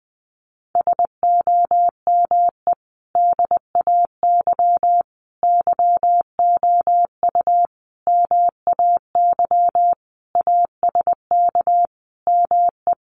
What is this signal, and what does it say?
Morse code, heard as a keyed tone, about 19 wpm, 700 Hz. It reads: SOME DAY YOU MAY ASK ME
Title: Sentences from Top 200 Words 20wpm